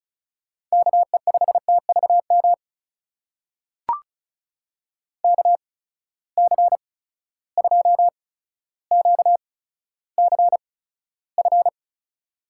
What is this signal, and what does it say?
Morse code, heard as a keyed tone, about 36 wpm, spaced as for 15 wpm, 700 Hz. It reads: KE5TVM KC2QCF
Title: USA Call Signs - 6 Characters - 8x Character Spacing - 35wpm